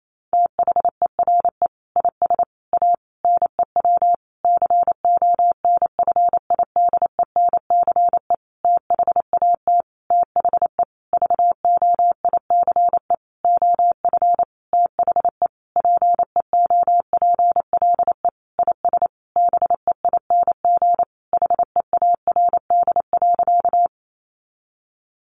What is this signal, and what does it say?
Morse code, heard as a keyed tone, about 28 wpm, 700 Hz. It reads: THERE IS A NEW CONFIDENCE THAT THE VOICE OF THE PEOPLE IS BEING HEARD.
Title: Sentences from Top 1500 Words 28wpm